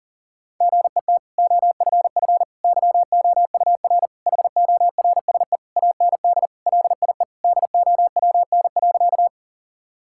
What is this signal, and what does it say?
Morse code, heard as a keyed tone, about 40 wpm, 700 Hz. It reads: GET OFF YOUR HORSE AND LIE DOWN.
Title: Sentences from Top 1600 Words 40wpm